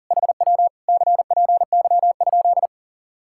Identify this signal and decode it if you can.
Morse code, heard as a keyed tone, about 47 wpm, 700 Hz. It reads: HW CPY?
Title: Speed-Racing Common QSO Phrases - 40wpm